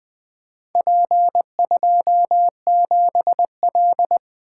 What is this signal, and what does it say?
Morse code, heard as a keyed tone, about 24 wpm, 700 Hz. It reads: P27L
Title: Speed-Racing - Four Letter-Number Combinations - 20wpm